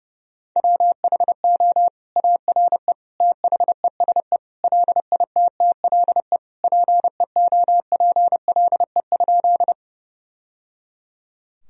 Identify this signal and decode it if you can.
Morse code, heard as a keyed tone, about 29 wpm, 700 Hz. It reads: WHO ARE THESE LITTLE PEOPLE?
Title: Sentences from Top 1000 Words - Review Part 1 - 30wpm